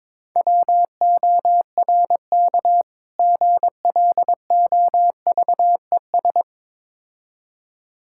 Morse code, heard as a keyed tone, about 23 wpm, 690 Hz. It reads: WORK GLOVES
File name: Difficult Words with Hints - 22wpm